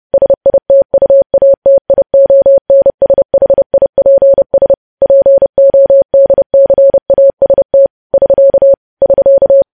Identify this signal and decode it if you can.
Morse code, heard as a keyed tone, about 30 wpm, 550 Hz. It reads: SITUATIONSHIPS PODCAST <SK> <SK>
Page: news-headlines-in-morse-code-at-30-wpm